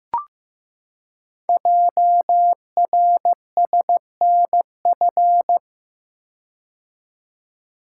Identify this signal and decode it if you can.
Morse code, heard as a keyed tone, about 15 wpm, 700 Hz. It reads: JRSNF
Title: Code Groups - 5 Letters - 15wpm